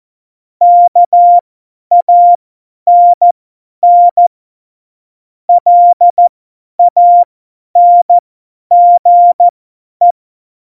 Morse code, heard as a keyed tone, about 14 wpm, spaced as for 10 wpm, 700 Hz. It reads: KANN LANGE
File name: Morsekurs / Witze / Witz 15. Mai 2018, 50 F70